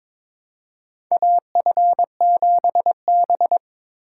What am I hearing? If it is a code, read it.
Morse code, heard as a keyed tone, about 22 wpm, 700 Hz. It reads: AF7B
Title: USA Call Signs - 4 Characters - 3x Character Spacing - 22wpm